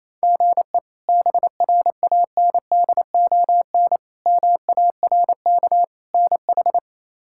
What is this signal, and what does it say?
Morse code, heard as a keyed tone, about 27 wpm, 700 Hz. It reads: GE BRANDON MARK NH
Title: K1USN SST QSO Practice - 28wpm